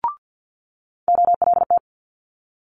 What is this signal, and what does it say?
Morse code, heard as a keyed tone, about 50 wpm, 700 Hz. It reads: OLT
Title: Multi-letter Phonemes Quick Fire 50wpm